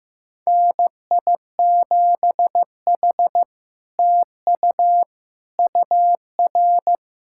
Morse code, heard as a keyed tone, about 15 wpm, 700 Hz. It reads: NI7H TU UR
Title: SOTA QSO Practice 15wpm